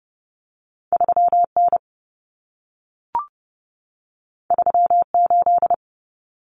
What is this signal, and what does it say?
Morse code, heard as a keyed tone, about 30 wpm, 700 Hz. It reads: 3N 38
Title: Common Cut Numbers - 2-Digit 30wpm